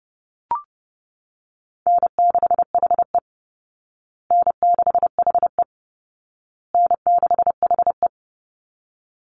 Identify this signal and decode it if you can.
Morse code, heard as a keyed tone, about 30 wpm, 700 Hz. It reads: N6HE N6HE N6HE
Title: Easy Call Signs Repeated 3x - 30wpm